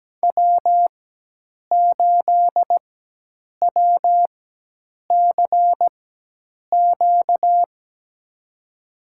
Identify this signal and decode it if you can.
Morse code, heard as a keyed tone, about 17 wpm, spaced as for 8 wpm, 700 Hz. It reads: W8WCQ
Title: USA Call Signs - 5 Characters - 4x Character Spacing - 17wpm